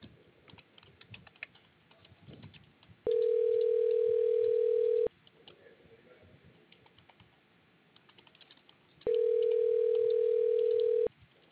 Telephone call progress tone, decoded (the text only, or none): ringback tone